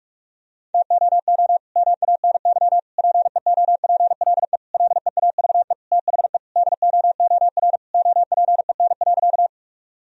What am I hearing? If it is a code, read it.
Morse code, heard as a keyed tone, about 46 wpm, 700 Hz. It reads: TOO MANY PEOPLE LEAVE THE DOOR OPEN.
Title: Sentences from Top 1000 Words - Review Part 1 - 45wpm